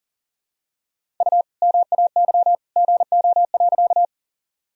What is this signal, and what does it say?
Morse code, heard as a keyed tone, about 40 wpm, 700 Hz. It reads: U MAY GO.
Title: Sentences from Top 100 Words and Abbreviations 40wpm